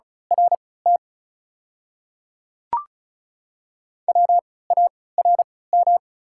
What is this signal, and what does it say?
Morse code, heard as a keyed tone, about 34 wpm, spaced as for 20 wpm, 690 Hz. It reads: RT WARM
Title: 3x Character Spacing - Top 500 Words - 35wpm